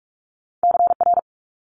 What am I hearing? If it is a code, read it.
Morse code, heard as a keyed tone, about 46 wpm, 700 Hz. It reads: CR
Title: Multi-letter Phonemes Quick Fire 45wpm